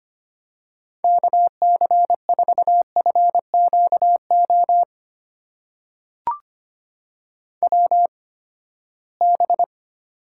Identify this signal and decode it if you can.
Morse code, heard as a keyed tone, about 25 wpm, spaced as for 17 wpm, 700 Hz. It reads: KC4FQO WB
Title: USA Call Signs - 6 Characters - 8x Character Spacing - 25wpm